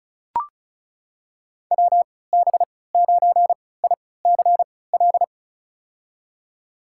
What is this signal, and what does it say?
Morse code, heard as a keyed tone, about 36 wpm, spaced as for 20 wpm, 700 Hz. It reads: WB9ICL
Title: USA Call Signs - 6 Characters - 3x Character Spacing - 35wpm